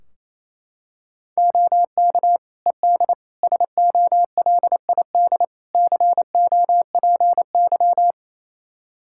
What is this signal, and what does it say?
Morse code, heard as a keyed tone, about 27 wpm, 700 Hz. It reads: OK ED SOLID COPY
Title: K1USN SST QSO Practice 28wpm